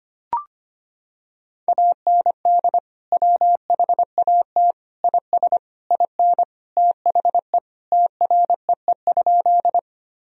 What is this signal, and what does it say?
Morse code, heard as a keyed tone, about 25 wpm, 700 Hz. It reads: AND WHAT IS IN THE TREE?
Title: Sentences from Top 1000 Words - Review Part 1 - 25wpm